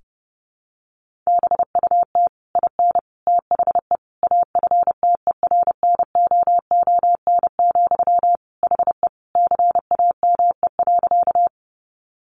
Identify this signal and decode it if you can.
Morse code, heard as a keyed tone, about 30 wpm, 700 Hz. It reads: BUT IN THE AFTERNOON, HE CAME.
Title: Sentences from Top 1500 Words 30wpm